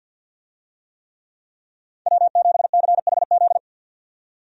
Morse code, heard as a keyed tone, about 50 wpm, 700 Hz. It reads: W7CHZ
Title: USA Call Signs - 5 Characters - 50wpm